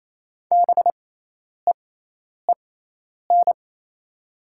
Morse code, heard as a keyed tone, about 28 wpm, spaced as for 10 wpm, 700 Hz. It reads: BEEN
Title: 6x Character Spacing - Top 100 Words - 28wpm